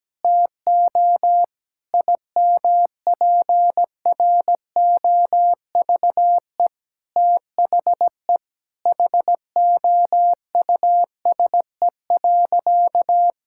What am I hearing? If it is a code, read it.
Morse code, heard as a keyed tone, about 17 wpm, 700 Hz. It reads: TO IMPROVE THE HOUSE.
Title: Sentences from Top 1000 Words - Review Part 5 - 17wpm